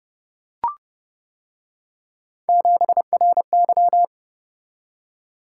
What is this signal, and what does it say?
Morse code, heard as a keyed tone, about 29 wpm, 700 Hz. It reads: 7RY